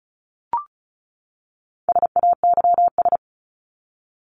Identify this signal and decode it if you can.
Morse code, heard as a keyed tone, about 34 wpm, 700 Hz. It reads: SAYS
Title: ES:S Suffix - Top 100 Words with Base Words 35wpm